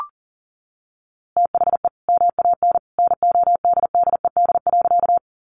Morse code, heard as a keyed tone, about 40 wpm, 700 Hz. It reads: THE MAN NODDED.
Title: A Sound Of Thunder 40wpm